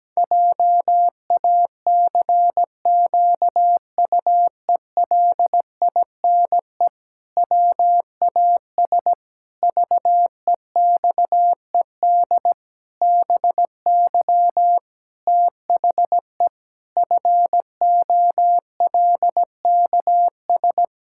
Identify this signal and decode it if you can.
Morse code, heard as a keyed tone, about 17 wpm, 700 Hz. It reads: JACQUELINE WAS VEXED BY THE FOLKS
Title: Pangram Challenge v2 - 17wpm